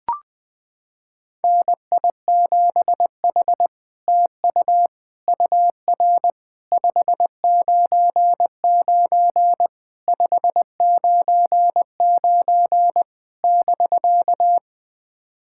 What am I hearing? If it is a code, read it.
Morse code, heard as a keyed tone, about 21 wpm, 700 Hz. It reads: NI7H TU UR 599 599 <BK>
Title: SOTA QSO Practice 20wpm